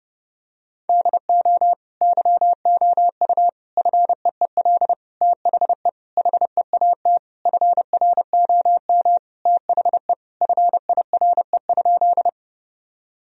Morse code, heard as a keyed tone, about 30 wpm, 700 Hz. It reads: DO YOU FEEL THE HEAT FROM THE FIRE?
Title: Sentences from Top 1000 Words - Review Part 3 - 30wpm